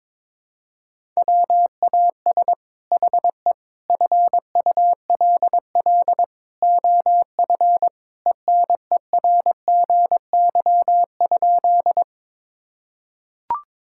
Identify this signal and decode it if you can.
Morse code, heard as a keyed tone, about 21 wpm, 700 Hz. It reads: WAS HE FULL OF ENERGY?
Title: Sentences from Top 1000 Words - Review Part 6 - 22wpm